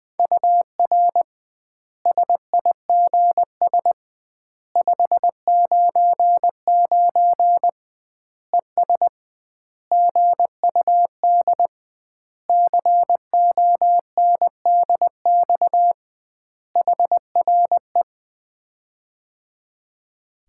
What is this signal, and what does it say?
Morse code, heard as a keyed tone, about 20 wpm, 700 Hz. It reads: UR SIGS 599 ES GUD CONDX HRE
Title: Common QSO Phrases - 2x Word-Spacing - 20wpm